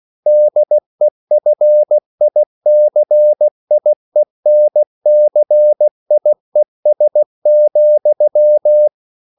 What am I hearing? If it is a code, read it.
Morse code, heard as a keyed tone, about 16 wpm, 600 Hz. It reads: DEFICIENCIES,
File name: CW / Practice / EnglishQuotations0216